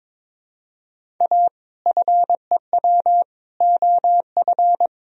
Morse code, heard as a keyed tone, about 21 wpm, 700 Hz. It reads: A FEW OF
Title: Sentences from Top 1000 Words - Review Part 1 - 22wpm